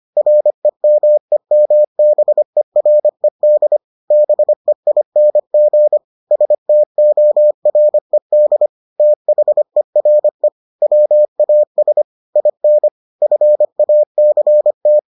Morse code, heard as a keyed tone, about 25 wpm, 600 Hz. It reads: REMEMBERED BEING STORED THERE WAS IN FACT